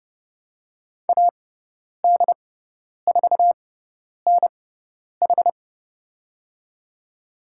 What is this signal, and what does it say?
Morse code, heard as a keyed tone, about 29 wpm, spaced as for 10 wpm, 700 Hz. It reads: AD4NH
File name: Lesson 16g - Letter c - Call Signs 30wpm